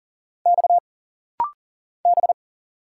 Morse code, heard as a keyed tone, about 40 wpm, 700 Hz. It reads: X B
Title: Single Letters - Rapid-Fire 40wpm